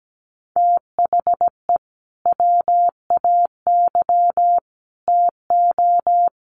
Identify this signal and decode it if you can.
Morse code, heard as a keyed tone, about 17 wpm, 700 Hz. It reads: THE WAY TO
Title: Sets of 3 Words 17wpm